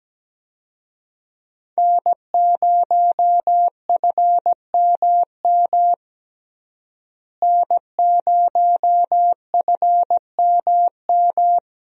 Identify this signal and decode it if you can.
Morse code, heard as a keyed tone, about 17 wpm, 700 Hz. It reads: N0FMM N0FMM
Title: USA Call Signs - 5 Characters - Repeated 3x - 17wpm